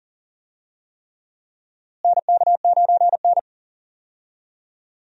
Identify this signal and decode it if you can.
Morse code, heard as a keyed tone, about 40 wpm, 700 Hz. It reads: NK9N
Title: USA Call Signs - 4 Characters - Repeated 3x - 40wpm